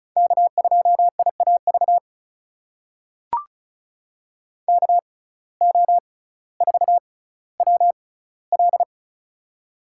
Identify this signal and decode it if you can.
Morse code, heard as a keyed tone, about 36 wpm, spaced as for 18 wpm, 700 Hz. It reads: K2IAV KO4WL
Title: USA Call Signs - 5 Characters - 6x Character Spacing - 35wpm